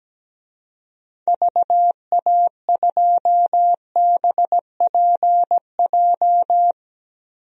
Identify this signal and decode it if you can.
Morse code, heard as a keyed tone, about 17 wpm, 700 Hz. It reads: VA2BPJ
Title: DXCC-100 Call Signs - 6 Characters - Repeated 3x - 17wpm